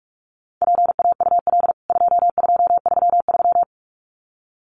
Morse code, heard as a keyed tone, about 44 wpm, 700 Hz. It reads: PAUL 2233
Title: CWT Exchange 45wpm